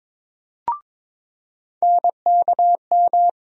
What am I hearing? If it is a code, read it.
Morse code, heard as a keyed tone, about 22 wpm, 700 Hz. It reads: NKM
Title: USA Call Sign Postfixes - 3 Letters 22wpm